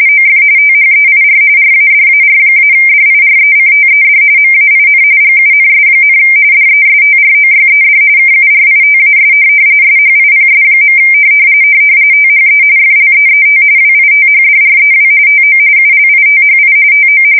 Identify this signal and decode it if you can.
RTTY, about 45.45 baud. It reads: OF THEIR RADAR-DIRECTED MAIN GUNS, DESTROYED KURITAS SHIPS. THE FLEET WITHDREW TO ULITHI TO
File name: fsk170 / USS-Massachusetts